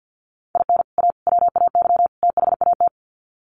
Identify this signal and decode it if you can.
Morse code, heard as a keyed tone, about 50 wpm, 700 Hz. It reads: IN A WAY THAT
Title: Sets of 4 Words 50wpm